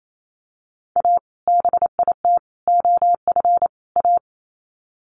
Morse code, heard as a keyed tone, about 28 wpm, 700 Hz. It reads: A BIT OF A
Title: Sets of 4 Words 28wpm